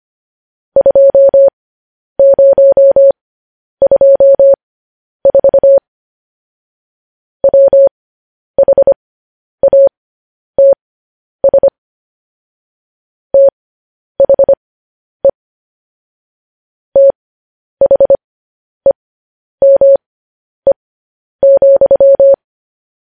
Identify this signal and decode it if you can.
Morse code, heard as a keyed tone, about 25 wpm, spaced as for 10 wpm, 550 Hz. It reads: 2024 WHATS THE THEME,